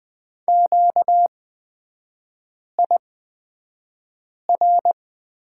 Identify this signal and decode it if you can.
Morse code, heard as a keyed tone, about 20 wpm, spaced as for 5 wpm, 690 Hz. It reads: QIR
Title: 236 Lesson 38e - Letter q - Three Characters 20wpm